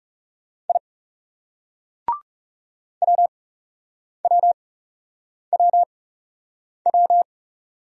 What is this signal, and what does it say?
Morse code, heard as a keyed tone, about 39 wpm, spaced as for 8 wpm, 700 Hz. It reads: I WWWW